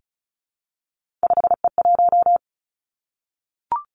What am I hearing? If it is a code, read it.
Morse code, heard as a keyed tone, about 36 wpm, 700 Hz. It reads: 5E1
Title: Three Challenging Letter-Number Combinations 35wpm